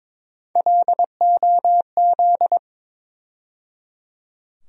Morse code, heard as a keyed tone, about 22 wpm, 700 Hz. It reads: LOZ
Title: USA Call Sign Postfixes - 3 Letters 22wpm